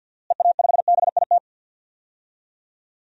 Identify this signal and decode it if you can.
Morse code, heard as a keyed tone, about 52 wpm, 700 Hz. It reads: EA5BIT
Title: DXCC-100 Call Signs - 6 Characters - Repeated 2x - 50wpm